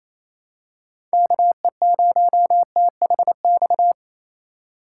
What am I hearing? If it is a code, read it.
Morse code, heard as a keyed tone, about 28 wpm, 700 Hz. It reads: KE0THX
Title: USA Call Signs - 3x Character Spacing - 28wpm